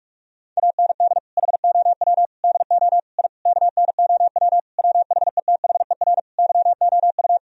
Morse code, heard as a keyed tone, about 46 wpm, 700 Hz. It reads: AND HOW DO I KNOW WHETHER YOU
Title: Sentences from Top 1000 Words - Review Part 3 - 45wpm